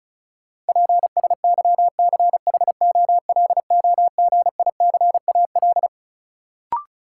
Morse code, heard as a keyed tone, about 36 wpm, 700 Hz. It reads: PSYCHOLOGICAL
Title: Sentences from Top 1400 Words - New Word Drill - 35wpm